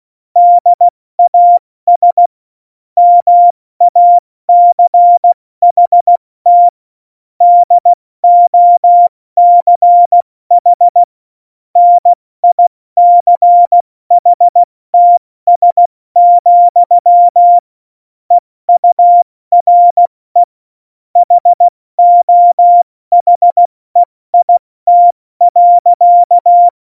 Morse code, heard as a keyed tone, about 16 wpm, 700 Hz. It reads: DAS MACHT DOCH NICHTS, EURE HOHEIT.
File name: Morsekurs / Witze / Witz 19 Mai, 2020 70 F80